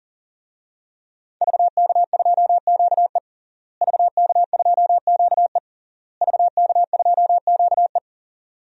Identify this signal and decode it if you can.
Morse code, heard as a keyed tone, about 40 wpm, 700 Hz. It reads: VK2QE VK2QE VK2QE
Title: DXCC-100 Call Signs - 5 Characters - Repeated 3x - 40wpm